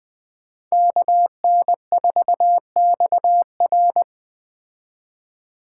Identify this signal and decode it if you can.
Morse code, heard as a keyed tone, about 20 wpm, 700 Hz. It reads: KN4XR
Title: Speed Racing - USA 2x2 Call Signs 17wpm